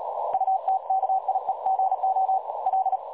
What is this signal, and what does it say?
Morse code, heard as a keyed tone, about 36 wpm, 760 Hz, with noise. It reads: UTKIEGO L
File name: dlazaawansowanych / nagraniazpasma / websdr recording 3565.0kHz